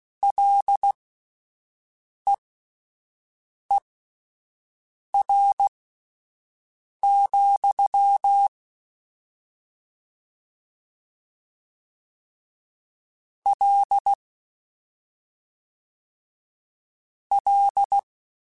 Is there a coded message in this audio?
Yes: Morse code from 0.2 to 18.0 s, heard as a keyed tone, about 16 wpm, spaced as for 5 wpm, 800 Hz. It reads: LEER, L L